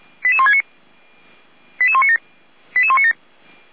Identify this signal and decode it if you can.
Five-tone selective call (ZVEI), heard as quick runs of five tones, tones of about 70 ms. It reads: 88169 88197 88197